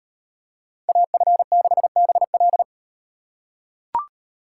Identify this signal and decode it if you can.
Morse code, heard as a keyed tone, about 38 wpm, 700 Hz. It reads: AF6BL